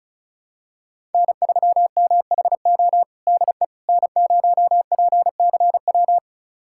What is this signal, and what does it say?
Morse code, heard as a keyed tone, about 34 wpm, 700 Hz. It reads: N3MHO DE N0PCW
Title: USA Callsign DE Callsign - 5 Characters - 35wpm